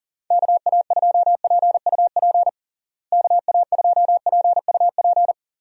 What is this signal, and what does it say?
Morse code, heard as a keyed tone, about 40 wpm, 700 Hz. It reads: KA2PUP KA2PUP
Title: USA Call Signs - 6 Characters - Repeated 3x - 40wpm